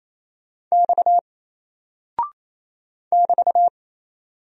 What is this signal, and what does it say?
Morse code, heard as a keyed tone, about 27 wpm, 700 Hz. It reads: X <BT>
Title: Single Character (Letter:Number:Punctuation:Prosign) 28wpm